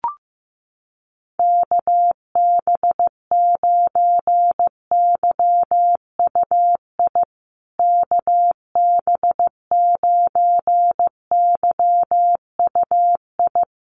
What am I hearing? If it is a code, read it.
Morse code, heard as a keyed tone, about 15 wpm, 700 Hz. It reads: KB9YUI KB9YUI
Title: USA Call Signs Repeated 3x 15wpm